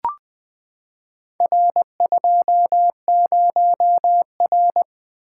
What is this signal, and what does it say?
Morse code, heard as a keyed tone, about 20 wpm, 700 Hz. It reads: R20R
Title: Four Letter-Number Combinations - 20wpm